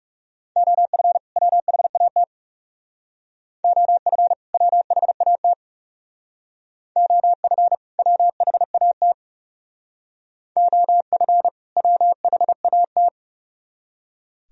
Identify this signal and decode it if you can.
Morse code, heard as a keyed tone, about 38 wpm, 700 Hz. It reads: OF WHAT OF WHAT OF WHAT OF WHAT